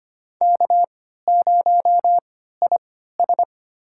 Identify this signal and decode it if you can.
Morse code, heard as a keyed tone, about 25 wpm, spaced as for 14 wpm, 700 Hz. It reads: K0IS
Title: USA Call Signs - 4 Characters - 3x Character Spacing - 25wpm